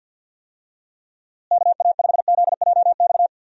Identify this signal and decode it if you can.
Morse code, heard as a keyed tone, about 49 wpm, 700 Hz. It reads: KA5ZJX
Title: USA Call Signs - 6 Characters - Repeated 3x - 50wpm